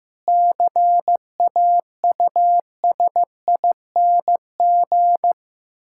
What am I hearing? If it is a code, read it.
Morse code, heard as a keyed tone, about 15 wpm, 700 Hz. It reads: CAUSING
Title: Pangram Challenge v2 - 15wpm